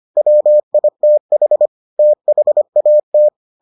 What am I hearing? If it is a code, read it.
Morse code, heard as a keyed tone, about 25 wpm, 600 Hz. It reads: WITH THAT